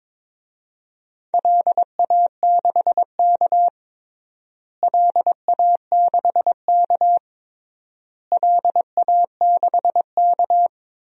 Morse code, heard as a keyed tone, about 22 wpm, 700 Hz. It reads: LA6K LA6K LA6K
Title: DXCC-100 Call Signs - 4 Characters - Repeated 3x - 22wpm